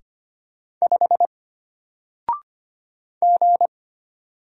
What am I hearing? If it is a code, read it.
Morse code, heard as a keyed tone, about 25 wpm, 700 Hz. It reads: 5 G
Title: Single Letter-Number Alt 25wpm